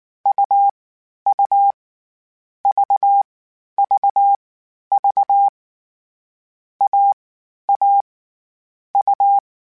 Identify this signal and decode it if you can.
Morse code, heard as a keyed tone, about 19 wpm, 800 Hz. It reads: U U V V V A A U